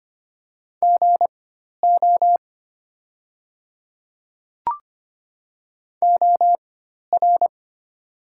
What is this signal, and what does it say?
Morse code, heard as a keyed tone, about 25 wpm, spaced as for 12 wpm, 700 Hz. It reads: GO OR
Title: 4x Character Spacing - Top 200 Words - 25wpm